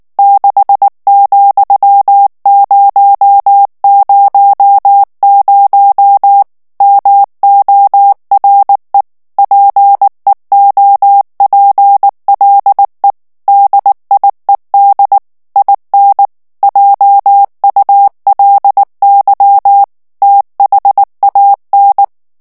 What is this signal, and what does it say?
Morse code, heard as a keyed tone, about 19 wpm, 800 Hz. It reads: 6,000 MORE PEOPLE DIED IN JULY THAN
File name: ANGLAIS texte clair 2 vitesse 18 mots minute